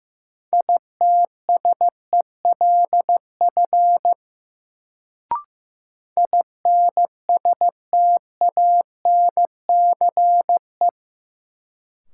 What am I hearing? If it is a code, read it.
Morse code, heard as a keyed tone, about 15 wpm, 700 Hz. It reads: ITSELF INSTANCE
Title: Sentences from Top 800 Words - New Word Drill - 15wpm